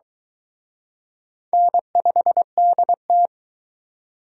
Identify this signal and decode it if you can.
Morse code, heard as a keyed tone, about 22 wpm, 700 Hz. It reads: N5DT